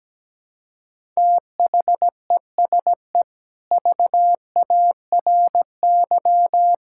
Morse code, heard as a keyed tone, about 17 wpm, 700 Hz. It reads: THESE VARY
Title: Sentences from Top 1600 Words 17wpm